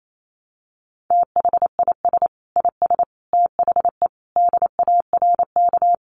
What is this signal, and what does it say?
Morse code, heard as a keyed tone, about 28 wpm, 700 Hz. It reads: THIS IS THE DARK
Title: Sentences from Top 1200 Words 28wpm